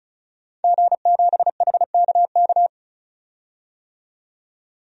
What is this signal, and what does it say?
Morse code, heard as a keyed tone, about 34 wpm, 700 Hz. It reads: G7HKK
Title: DXCC-100 Call Signs - 5 Characters - 35wpm